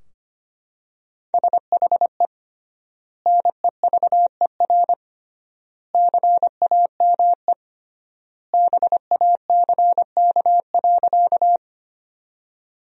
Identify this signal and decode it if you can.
Morse code, heard as a keyed tone, about 25 wpm, 700 Hz. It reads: SHE NEVER CAME BACK.
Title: 3x Word-Spacing - Sentences from Top 300 Words - 25wpm